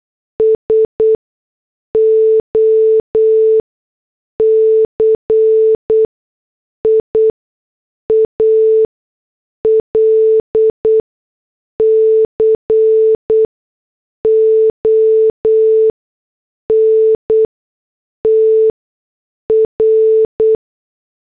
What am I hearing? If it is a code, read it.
Morse code, heard as a keyed tone, about 8 wpm, spaced as for 6 wpm, 440 Hz. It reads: SOCIAL CONTR